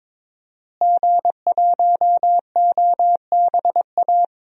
Morse code, heard as a keyed tone, about 22 wpm, 700 Hz. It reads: G1OBA
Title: DXCC-100 Call Signs - 5 Characters - Repeated 3x - 22wpm